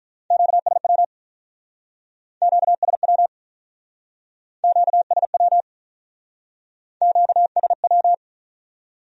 Morse code, heard as a keyed tone, about 43 wpm, 700 Hz. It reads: QSW QSW QSW QSW
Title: Speed-Racing Q-Codes - 35wpm